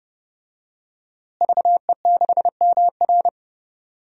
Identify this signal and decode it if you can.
Morse code, heard as a keyed tone, about 32 wpm, 700 Hz. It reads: VE6MR
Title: DXCC-100 Call Signs - 5 Characters - Repeated 3x - 30wpm